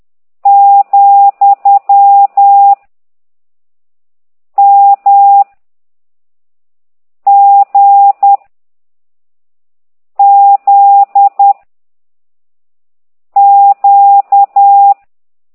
Morse code, heard as a keyed tone, about 10 wpm, 800 Hz. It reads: , M G Z Q